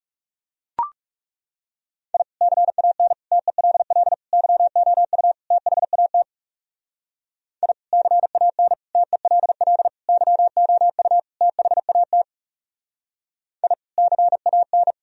Morse code, heard as a keyed tone, about 38 wpm, 700 Hz. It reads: I CAN TELL YOU THAT I CAN TELL YOU THAT I CAN